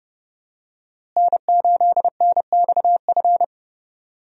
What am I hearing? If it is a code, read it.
Morse code, heard as a keyed tone, about 30 wpm, 700 Hz. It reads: N8NXF
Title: USA Call Signs - 5 Characters - Repeated 2x - 30wpm